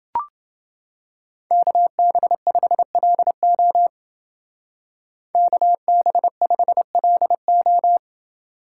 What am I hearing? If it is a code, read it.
Morse code, heard as a keyed tone, about 29 wpm, 700 Hz. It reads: KB5LO KB5LO